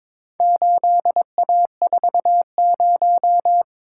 Morse code, heard as a keyed tone, about 22 wpm, 700 Hz. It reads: 8A40
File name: practice / 22wpm / Four Letter-Number Combinations - 22wpm